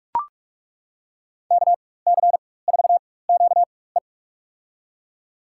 Morse code, heard as a keyed tone, about 41 wpm, spaced as for 20 wpm, 700 Hz. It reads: KC4QE
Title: USA Call Signs - 5 Characters - 4x Character Spacing - 45wpm